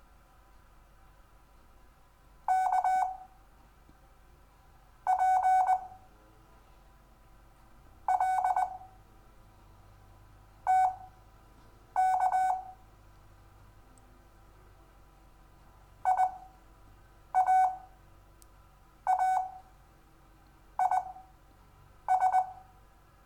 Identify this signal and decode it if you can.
Morse code, heard as a keyed tone, about 20 wpm, spaced as for 6 wpm, 750 Hz. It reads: K P L TK IAAIS
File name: cw / Lesson6 WI